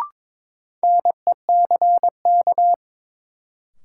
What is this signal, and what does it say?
Morse code, heard as a keyed tone, about 22 wpm, 700 Hz. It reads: NECK
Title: Sets of 5 Words Encore - New Word Drill - 22wpm